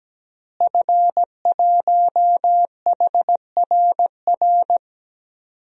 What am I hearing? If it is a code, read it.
Morse code, heard as a keyed tone, about 17 wpm, 700 Hz. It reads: F1HRR
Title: DXCC-100 Call Signs - 5 Characters - Repeated 3x - 17wpm